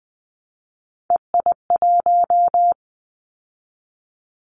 Morse code, heard as a keyed tone, about 19 wpm, 700 Hz. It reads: EI1
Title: Callsign Prefixes for Top 50 DXCC - 20wpm